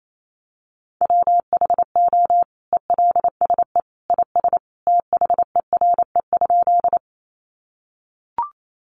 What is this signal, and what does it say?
Morse code, heard as a keyed tone, about 28 wpm, 700 Hz. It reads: WHO ELSE IS THERE?
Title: Sentences from Top 800 Words 28wpm